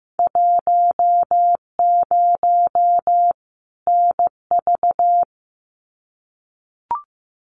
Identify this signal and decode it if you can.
Morse code, heard as a keyed tone, about 15 wpm, 700 Hz. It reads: 10 NV
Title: Winter Field Day Exchanges 15wpm